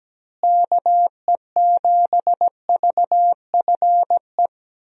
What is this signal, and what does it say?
Morse code, heard as a keyed tone, about 17 wpm, 700 Hz. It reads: KE7VFE
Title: USA Call Signs - 6 Characters - Repeated 3x - 17wpm